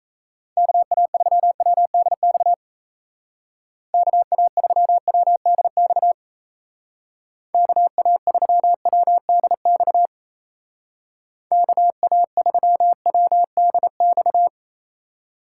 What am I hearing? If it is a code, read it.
Morse code, heard as a keyed tone, about 35 wpm, 700 Hz. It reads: KA3WDX KA3WDX KA3WDX KA3WDX